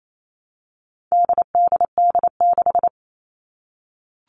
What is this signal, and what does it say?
Morse code, heard as a keyed tone, about 28 wpm, 700 Hz. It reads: DDD6
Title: All Cut Numbers - 4-Digit 28wpm